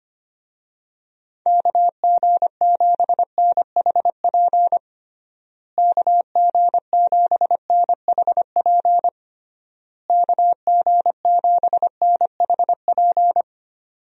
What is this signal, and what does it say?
Morse code, heard as a keyed tone, about 25 wpm, 700 Hz. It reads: KG7NHP KG7NHP KG7NHP
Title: USA Call Signs - 6 Characters - Repeated 3x - 25wpm